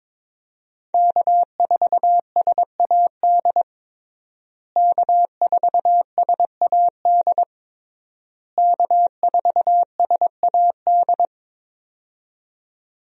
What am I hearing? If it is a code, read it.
Morse code, heard as a keyed tone, about 22 wpm, 700 Hz. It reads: K4SAD K4SAD K4SAD
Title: USA Call Signs - 5 Characters - Repeated 3x - 22wpm